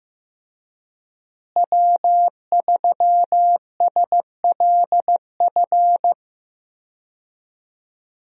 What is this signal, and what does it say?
Morse code, heard as a keyed tone, about 15 wpm, 700 Hz. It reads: W3SLF
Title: USA Call Signs - 5 Characters - Repeated 3x - 15wpm